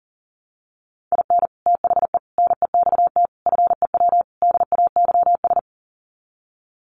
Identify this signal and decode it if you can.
Morse code, heard as a keyed tone, about 37 wpm, 700 Hz. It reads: IN THE NEXT FEW DAYS
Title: Sets of 5 Words Encore2 40wpm